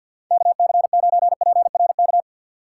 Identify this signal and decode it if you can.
Morse code, heard as a keyed tone, about 56 wpm, 700 Hz. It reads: KC9PRK
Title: Speed Racing - USA 2x3 Call Signs 50wpm